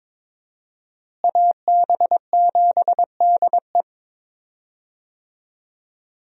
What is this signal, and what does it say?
Morse code, heard as a keyed tone, about 22 wpm, 700 Hz. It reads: AB7DE
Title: USA Call Signs - 5 Characters - Repeated 3x - 22wpm